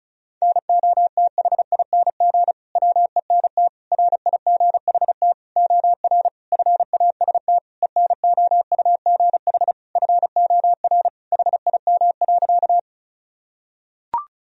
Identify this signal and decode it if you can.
Morse code, heard as a keyed tone, about 36 wpm, 700 Hz. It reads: NOTHING WENT RIGHT OR FAST ENOUGH FOR HIM.
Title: Sentences from Top 1000 Words - Review Part 2 - 35wpm